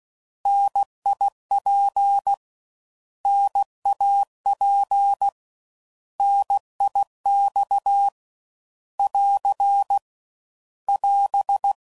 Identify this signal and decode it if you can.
Morse code, heard as a keyed tone, about 16 wpm, 800 Hz. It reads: NIP NAP NIX <AR> <AS>
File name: Mixed / Contrasts / 3LtrContrasts